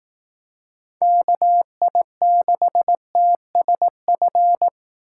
Morse code, heard as a keyed tone, about 18 wpm, 700 Hz. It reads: KI6TSF
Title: Speed Racing - USA 2x3 Call Signs 15wpm